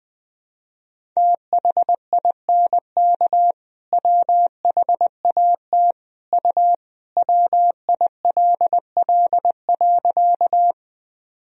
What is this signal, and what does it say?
Morse code, heard as a keyed tone, about 21 wpm, 700 Hz. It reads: THINK WHAT U WILL.
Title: Sentences from Top 200 Words and Abbreviations 20wpm